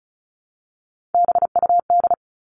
Morse code, heard as a keyed tone, about 34 wpm, 700 Hz. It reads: BUD
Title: All Cut Numbers - 3-Digit 35wpm